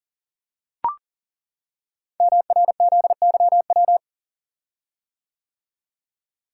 Morse code, heard as a keyed tone, about 42 wpm, 700 Hz. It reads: MRZYW
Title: Code Groups - 5 Letters - 40wpm